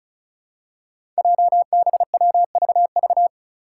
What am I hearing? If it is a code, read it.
Morse code, heard as a keyed tone, about 36 wpm, 700 Hz. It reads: JBWVV
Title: Five Challenging Letter-Number Combinations 35wpm